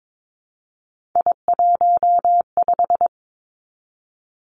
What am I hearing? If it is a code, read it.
Morse code, heard as a keyed tone, about 22 wpm, 700 Hz. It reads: I15
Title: Three Challenging Letter-Number Combinations 22wpm